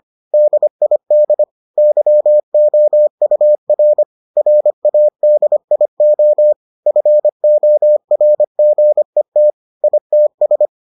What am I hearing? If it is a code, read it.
Morse code, heard as a keyed tone, about 25 wpm, 600 Hz. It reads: DID YOUR RADIO FORGET ITS